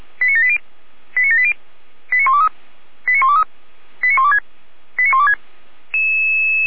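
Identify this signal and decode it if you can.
Five-tone selective call (ZVEI), heard as quick runs of five tones, tones of about 70 ms. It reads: 89780 89780 89123 89123 89126 89126
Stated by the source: ZVEI five-tone selective call received by SDR, fire brigade (Feuerwehr) Orlamünde.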